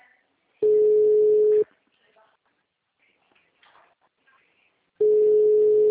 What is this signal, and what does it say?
Ringback tone (Europe, 425 Hz), with speech.
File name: out- -2006- - - .21371